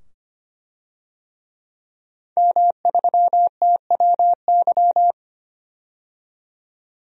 Morse code, heard as a keyed tone, about 25 wpm, 700 Hz. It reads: M3TWY
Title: DXCC-100 Call Signs - 5 Characters - 25wpm